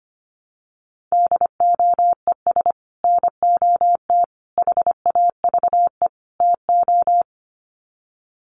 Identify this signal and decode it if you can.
Morse code, heard as a keyed tone, about 25 wpm, 700 Hz. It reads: DOES NOT HAVE TO
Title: Sets of 4 Words 25wpm